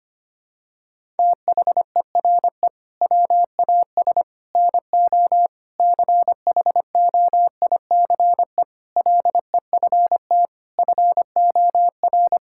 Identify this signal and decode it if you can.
Morse code, heard as a keyed tone, about 25 wpm, 700 Hz. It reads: THERE WAS NO CHOICE LEFT FOR
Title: Sentences from Top 1000 Words - Review Part 6 - 25wpm